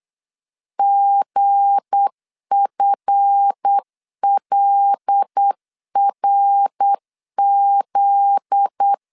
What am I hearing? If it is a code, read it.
Morse code, heard as a keyed tone, about 8 wpm, 800 Hz. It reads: GFLRZ
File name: LEZIONI / 7 GF9Z / 7 GF9Z 8wpm 3213